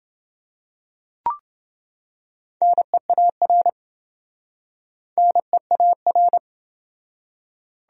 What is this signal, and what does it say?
Morse code, heard as a keyed tone, about 28 wpm, 700 Hz. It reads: NEAR NEAR